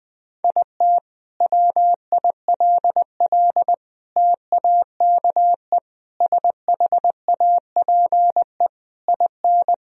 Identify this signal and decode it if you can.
Morse code, heard as a keyed tone, about 20 wpm, 700 Hz. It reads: IT WILL TAKE SHAPE IN
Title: Sentences from Top 1000 Words - Review Part 3 - 20wpm